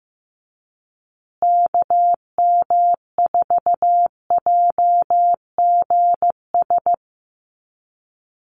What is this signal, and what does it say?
Morse code, heard as a keyed tone, about 15 wpm, 700 Hz. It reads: KM4JGS
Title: USA Call Signs Repeated 3x 15wpm